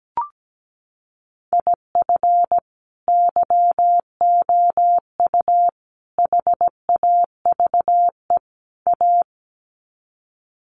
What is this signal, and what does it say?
Morse code, heard as a keyed tone, about 17 wpm, 700 Hz. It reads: IF YOU HAVE A
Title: Sets of 4 Words 17wpm